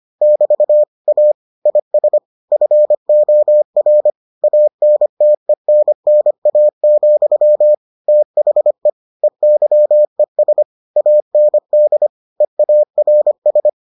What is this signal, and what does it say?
Morse code, heard as a keyed tone, about 25 wpm, 600 Hz. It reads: <BT> A IS FOR ANTENNA, THE EYES AND EARS